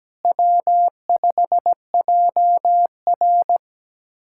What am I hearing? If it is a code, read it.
Morse code, heard as a keyed tone, about 17 wpm, 700 Hz. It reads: W5JR
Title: USA Callsign DE Callsign 17wpm